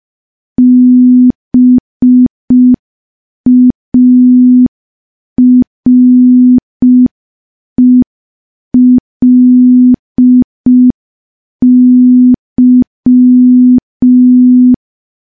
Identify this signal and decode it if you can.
Morse code, heard as a keyed tone, about 5 wpm, 250 Hz. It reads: BARELY
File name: morse / morse-training-05WPM